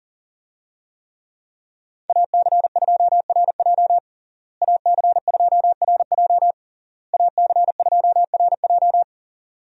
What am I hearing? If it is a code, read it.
Morse code, heard as a keyed tone, about 40 wpm, 700 Hz. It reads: AC2RJ AC2RJ AC2RJ
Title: USA Call Signs - 5 Characters - Repeated 3x - 40wpm